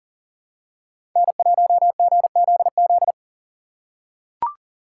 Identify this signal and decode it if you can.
Morse code, heard as a keyed tone, about 38 wpm, 700 Hz. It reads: N1GZZ
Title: USA Call Signs - 4x Character Spacing - 40wpm